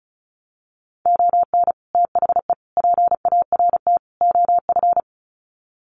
Morse code, heard as a keyed tone, about 34 wpm, 700 Hz. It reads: ON THE PART OF
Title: Sets of 4 Words 35wpm